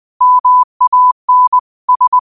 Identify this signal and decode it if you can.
Morse code, heard as a keyed tone, about 20 wpm, 1000 Hz. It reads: MAN S
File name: qotdfiles / 20wpm